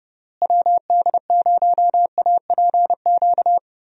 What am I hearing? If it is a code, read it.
Morse code, heard as a keyed tone, about 30 wpm, 700 Hz. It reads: WD0APQ
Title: USA Call Signs - 6 Characters - Repeated 3x - 30wpm